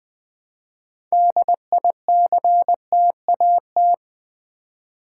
Morse code, heard as a keyed tone, about 20 wpm, 700 Hz. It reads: DICTAT
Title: Build A Word - 3 Letters At A Time - 8-Letter Words - 20wpm